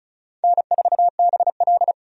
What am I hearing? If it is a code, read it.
Morse code, heard as a keyed tone, about 36 wpm, 700 Hz. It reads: N4BL
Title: USA Call Signs - 4 Characters - 35wpm